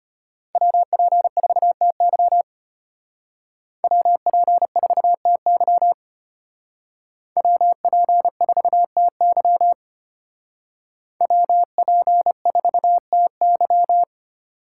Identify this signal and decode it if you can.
Morse code, heard as a keyed tone, about 32 wpm, 700 Hz. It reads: WP4TY WP4TY WP4TY WP4TY